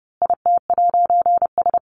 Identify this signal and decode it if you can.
Morse code, heard as a keyed tone, about 30 wpm, 700 Hz. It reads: IT'S
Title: A Sound Of Thunder 30wpm